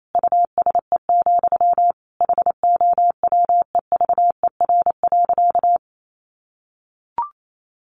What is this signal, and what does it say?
Morse code, heard as a keyed tone, about 28 wpm, 700 Hz. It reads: USE, HOWEVER.
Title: Sentences from Top 1400 Words 28wpm